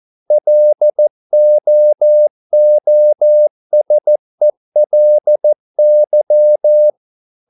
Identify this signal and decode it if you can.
Morse code, heard as a keyed tone, about 14 wpm, 600 Hz. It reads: LOOSELY